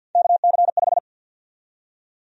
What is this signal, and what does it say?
Morse code, heard as a keyed tone, about 51 wpm, 700 Hz. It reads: KC5
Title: USA Call Sign Prefixes - 50wpm